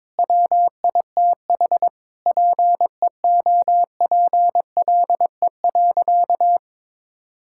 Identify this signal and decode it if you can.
Morse code, heard as a keyed tone, about 22 wpm, 700 Hz. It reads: WITH PEOPLE.
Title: Sentences from Top 1000 Words - Review Part 5 - 22wpm